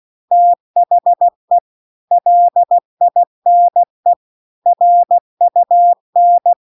Morse code, heard as a keyed tone, about 16 wpm, 700 Hz. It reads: THE LINE RUN